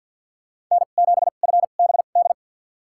Morse code, heard as a keyed tone, about 52 wpm, spaced as for 37 wpm, 700 Hz. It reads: N7FBD
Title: USA Call Signs - 5 Characters - 2x Character Spacing - 50wpm